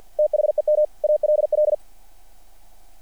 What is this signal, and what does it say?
Morse code, heard as a keyed tone, about 48 wpm, 600 Hz. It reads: THEM ALL
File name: archived / CRSnet